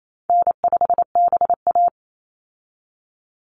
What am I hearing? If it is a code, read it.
Morse code, heard as a keyed tone, about 28 wpm, 700 Hz. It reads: N5BA
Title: Easy Call Signs - 28wpm